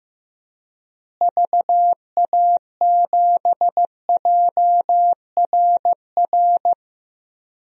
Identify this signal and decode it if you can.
Morse code, heard as a keyed tone, about 15 wpm, 700 Hz. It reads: VA7JRR
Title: DXCC-100 Call Signs - 6 Characters - Repeated 3x - 15wpm